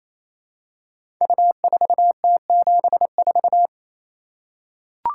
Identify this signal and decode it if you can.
Morse code, heard as a keyed tone, about 28 wpm, 700 Hz. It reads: U4T74
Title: Code Groups - 5 Letter:Numbers - 28wpm